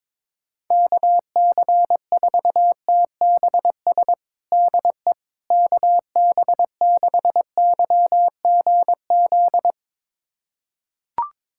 Morse code, heard as a keyed tone, about 21 wpm, 700 Hz. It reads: KC4TBS DE KB6YGZ
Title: USA Callsign DE Callsign - 6 Characters - 22wpm